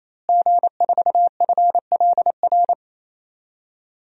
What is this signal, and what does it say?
Morse code, heard as a keyed tone, about 28 wpm, 700 Hz. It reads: G4FLR
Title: DXCC-100 Call Signs - 5 Characters - Repeated 3x - 28wpm